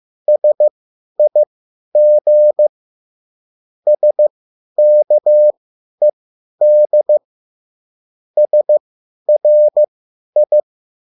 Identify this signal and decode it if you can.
Morse code, heard as a keyed tone, about 15 wpm, spaced as for 10 wpm, 600 Hz. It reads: SIG SKED SRI